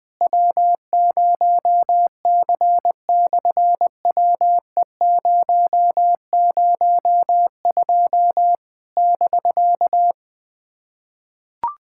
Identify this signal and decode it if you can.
Morse code, heard as a keyed tone, about 20 wpm, 700 Hz. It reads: W0C/WE002 <BK>
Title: SOTA QSO Practice 20wpm